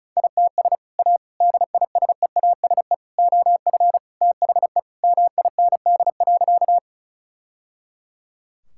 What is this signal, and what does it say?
Morse code, heard as a keyed tone, about 36 wpm, 700 Hz. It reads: ITS A DISEASE OF THE MIND.
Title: Sentences from Top 1000 Words - Review Part 6 - 35wpm